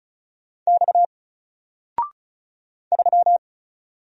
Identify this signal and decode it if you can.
Morse code, heard as a keyed tone, about 36 wpm, 700 Hz. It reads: X 3
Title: Single Letter-Number Alt 35wpm